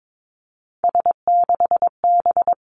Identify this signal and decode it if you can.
Morse code, heard as a keyed tone, about 22 wpm, 700 Hz. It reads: S6B
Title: Three Challenging Letter-Number Combinations 22wpm